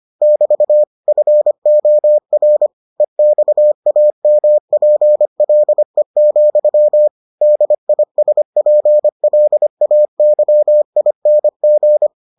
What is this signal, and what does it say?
Morse code, heard as a keyed tone, about 25 wpm, 600 Hz. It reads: <BT> FOR EXAMPLE, DISPLAYING